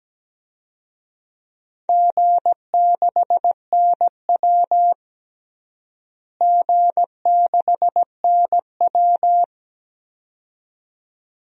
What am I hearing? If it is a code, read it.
Morse code, heard as a keyed tone, about 17 wpm, 700 Hz. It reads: G6NW G6NW
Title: DXCC-100 Call Signs - 4 Characters - Repeated 2x - 17wpm